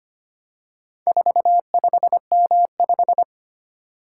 Morse code, heard as a keyed tone, about 25 wpm, 700 Hz. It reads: 45M5
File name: practice / 25wpm / Four Letter-Number Combinations - 25wpm